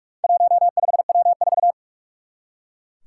Morse code, heard as a keyed tone, about 44 wpm, 700 Hz. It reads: 15W4
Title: Four Challenging Letter-Number Combinations 45wpm